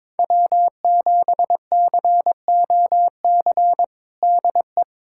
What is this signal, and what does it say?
Morse code, heard as a keyed tone, about 22 wpm, 700 Hz. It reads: W7COC DE
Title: USA Callsign DE Callsign - 5 Characters - 22wpm